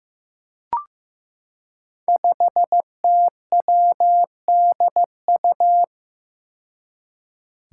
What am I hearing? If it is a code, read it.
Morse code, heard as a keyed tone, about 15 wpm, 700 Hz. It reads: 5TWDU
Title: Five Challenging Letter-Number Combinations 15wpm